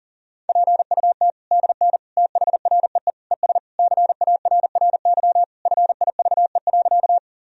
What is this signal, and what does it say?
Morse code, heard as a keyed tone, about 40 wpm, 700 Hz. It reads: PUT DN THREE ES CARRY FIVE.
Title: Sentences from Top 400 Words and Abbreviations 40wpm